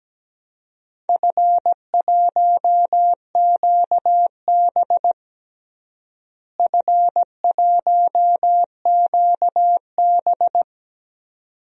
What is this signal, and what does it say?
Morse code, heard as a keyed tone, about 17 wpm, 700 Hz. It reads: F1QB F1QB
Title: DXCC-100 Call Signs - 4 Characters - Repeated 2x - 17wpm